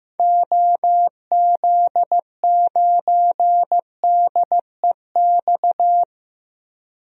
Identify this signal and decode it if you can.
Morse code, heard as a keyed tone, about 15 wpm, 700 Hz. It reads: OZ9DEX
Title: DXCC-100 Call Signs - 6 Characters - 15wpm